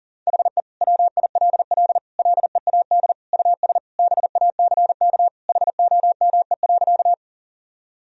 Morse code, heard as a keyed tone, about 37 wpm, 700 Hz. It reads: HE WILL LEAD US BACK HOME.
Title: Sentences from Top 1000 Words - Review Part 2 - 40wpm